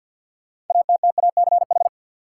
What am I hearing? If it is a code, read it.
Morse code, heard as a keyed tone, about 52 wpm, 700 Hz. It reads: ATTACH
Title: Six Letter Words - 50wpm